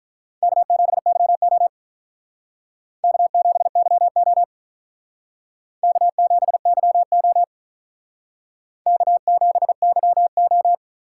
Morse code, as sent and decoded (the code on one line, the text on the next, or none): -.- --... -.-- --- / -.- --... -.-- --- / -.- --... -.-- --- / -.- --... -.-- ---
K7YO K7YO K7YO K7YO